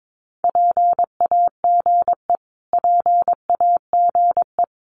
Morse code, heard as a keyed tone, about 22 wpm, 700 Hz. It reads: PAGE PAGE
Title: Top 300 Words Repeated 22wpm